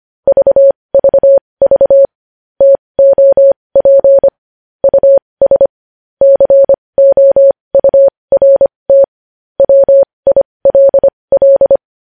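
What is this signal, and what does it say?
Morse code, heard as a keyed tone, about 25 wpm, spaced as for 20 wpm, 550 Hz. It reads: VVV TOP US COURT WILL
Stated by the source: Morse code practice sending news headlines at 20 wpm.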